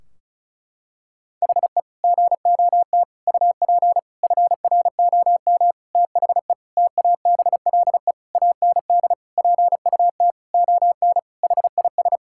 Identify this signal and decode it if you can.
Morse code, heard as a keyed tone, about 36 wpm, 700 Hz. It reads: HE GOT UP FROM THE TABLE AND PUT ON HIS
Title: Sentences from Top 1600 Words 35wpm